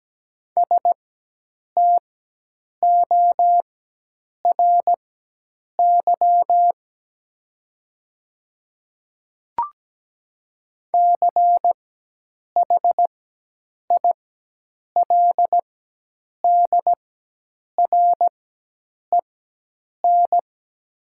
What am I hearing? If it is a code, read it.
Morse code, heard as a keyed tone, about 17 wpm, spaced as for 8 wpm, 690 Hz. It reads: STORY CHILDREN
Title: 4x Character Spacing - Top 300 Words - 17wpm